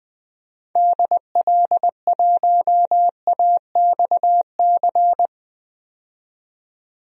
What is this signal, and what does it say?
Morse code, heard as a keyed tone, about 20 wpm, 700 Hz. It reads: DL1AXC
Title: DXCC-100 Call Signs - 6 Characters - 20wpm